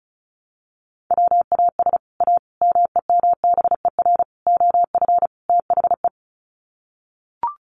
Morse code, heard as a keyed tone, about 34 wpm, 700 Hz. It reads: WAS A MEMBER OF THE
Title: Sets of 5 Words 35wpm